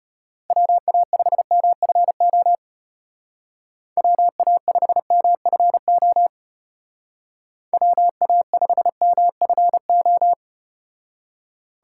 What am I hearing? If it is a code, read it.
Morse code, heard as a keyed tone, about 34 wpm, 700 Hz. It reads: WA5MFO WA5MFO WA5MFO